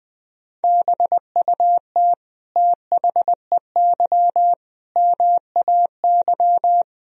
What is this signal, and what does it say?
Morse code, heard as a keyed tone, about 21 wpm, 700 Hz. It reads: BUT THEY MAY
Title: Sentences from Top 300 Words and Abbreviations 20wpm